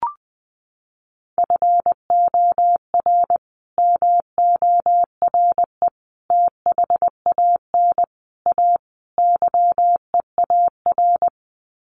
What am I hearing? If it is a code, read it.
Morse code, heard as a keyed tone, about 19 wpm, 700 Hz. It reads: FOR MORE THAN A YEAR
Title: Sets of 5 Words 20wpm